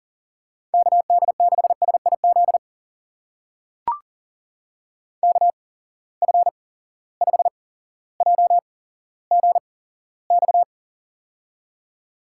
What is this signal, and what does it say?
Morse code, heard as a keyed tone, about 40 wpm, spaced as for 17 wpm, 700 Hz. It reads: KD6SIZ KF5JGX
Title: USA Call Signs - 6 Characters - 8x Character Spacing - 40wpm